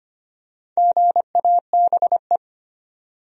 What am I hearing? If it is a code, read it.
Morse code, heard as a keyed tone, about 25 wpm, 700 Hz. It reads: GABE
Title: USA Common Names - 25wpm